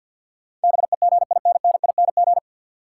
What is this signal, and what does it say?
Morse code, heard as a keyed tone, about 53 wpm, 700 Hz. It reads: BEGINNING
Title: Sets of 3 Words - New Word Drill - 50wpm